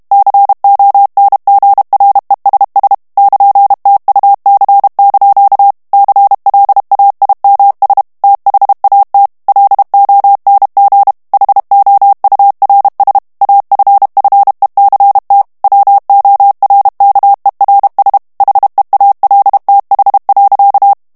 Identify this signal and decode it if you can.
Morse code, heard as a keyed tone, about 32 wpm, 800 Hz. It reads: CONGRESS <KN>TUC) CLAIMS THAT LONG HOURS AFFECT WORKERS HEALTH.
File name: morse mp3 / ANGLAIS texte clair 1 vitesse 30 mots minute